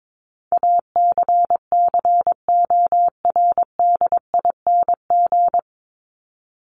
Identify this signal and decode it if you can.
Morse code, heard as a keyed tone, about 22 wpm, 700 Hz. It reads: ACCORDING
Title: LY Suffix - Top 100 Words with Base Words 22wpm